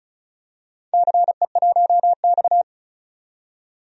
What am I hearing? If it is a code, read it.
Morse code, heard as a keyed tone, about 36 wpm, 690 Hz. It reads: CE1X
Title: DXCC-100 Call Signs - 4 Characters - 35wpm